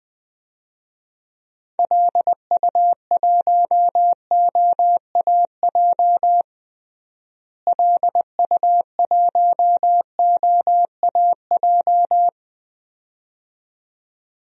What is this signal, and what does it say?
Morse code, heard as a keyed tone, about 20 wpm, 700 Hz. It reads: LU1OAJ LU1OAJ
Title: DXCC-100 Call Signs - 6 Characters - Repeated 2x - 20wpm